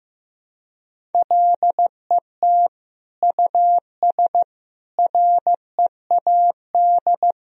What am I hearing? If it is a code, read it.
Morse code, heard as a keyed tone, about 15 wpm, 700 Hz. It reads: LET US READ
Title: Sentences from Top 1000 Words - Review Part 1 - 15wpm